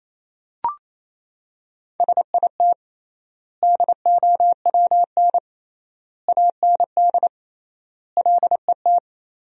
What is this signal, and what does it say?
Morse code, heard as a keyed tone, about 27 wpm, 700 Hz. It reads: SIT DOWN AND LET
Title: 3x Word-Spacing - Sentences from Top 500 Words - 28wpm